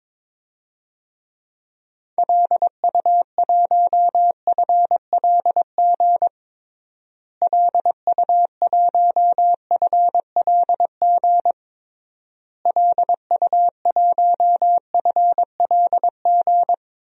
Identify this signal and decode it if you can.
Morse code, heard as a keyed tone, about 22 wpm, 700 Hz. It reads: LU1FLG LU1FLG LU1FLG
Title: DXCC-100 Call Signs - 6 Characters - Repeated 3x - 22wpm